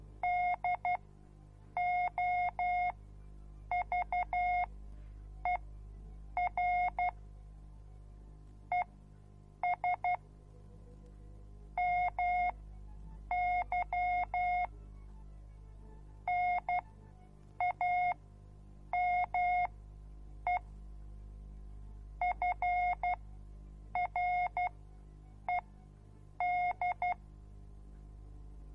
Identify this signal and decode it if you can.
Morse code, heard as a keyed tone, about 12 wpm, spaced as for 7 wpm, 720 Hz. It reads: DOVER ES MY NAME FRED